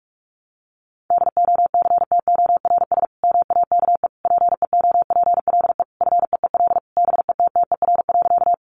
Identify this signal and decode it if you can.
Morse code, heard as a keyed tone, about 44 wpm, 700 Hz. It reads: DOCTORS MAKE PEOPLE FEEL BETTER.
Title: Easy Phrases 45wpm